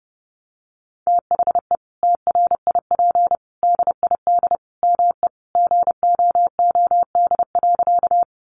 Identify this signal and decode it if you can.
Morse code, heard as a keyed tone, about 30 wpm, 700 Hz. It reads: THE TRIP DID ME GOOD.
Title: Sentences from Top 1300 Words 30wpm